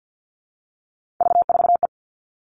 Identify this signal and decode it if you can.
Morse code, heard as a keyed tone, about 50 wpm, 700 Hz. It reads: V4E
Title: Three Challenging Letter-Number Combinations 50wpm